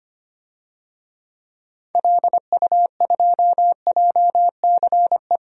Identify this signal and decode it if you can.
Morse code, heard as a keyed tone, about 25 wpm, 700 Hz. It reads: LU2JCE
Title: DXCC-100 Call Signs - 6 Characters - 25wpm